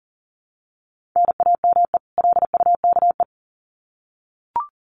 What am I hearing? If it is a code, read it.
Morse code, heard as a keyed tone, about 40 wpm, 700 Hz. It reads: NAME LUKE
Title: Common QSO Phrases 40wpm